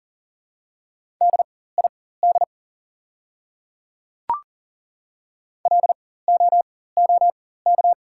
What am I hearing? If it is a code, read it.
Morse code, heard as a keyed tone, about 40 wpm, spaced as for 19 wpm, 700 Hz. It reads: DID LOOK
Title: 4x Character Spacing - Top 500 Words - 40wpm